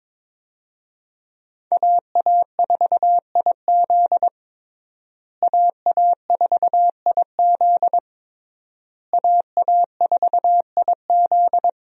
Morse code, heard as a keyed tone, about 22 wpm, 700 Hz. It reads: AA4IZ AA4IZ AA4IZ
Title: USA Call Signs - 5 Characters - Repeated 3x - 22wpm